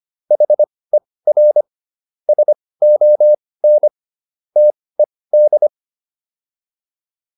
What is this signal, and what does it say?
Morse code, heard as a keyed tone, about 25 wpm, spaced as for 18 wpm, 600 Hz. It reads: HER SON TED
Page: intermediate-practice-files